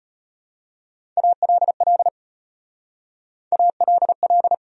Morse code, heard as a keyed tone, about 32 wpm, 700 Hz. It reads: ALL ALL